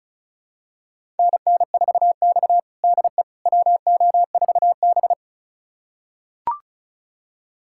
Morse code, heard as a keyed tone, about 36 wpm, 700 Hz. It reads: NN4X DE WO4B
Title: USA Callsign DE Callsign - 4 Characters - 35wpm